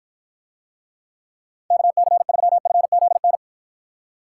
Morse code, heard as a keyed tone, about 49 wpm, 700 Hz. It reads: KC3FZN